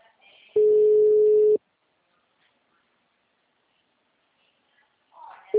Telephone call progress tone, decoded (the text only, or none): ringback tone